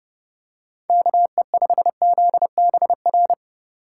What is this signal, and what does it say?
Morse code, heard as a keyed tone, about 30 wpm, 700 Hz. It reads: KE5ZBR